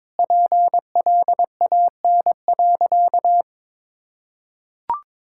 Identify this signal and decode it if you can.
Morse code, heard as a keyed tone, about 22 wpm, 700 Hz. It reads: PLAN.
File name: Sentences with 7-Word Phrases - 22wpm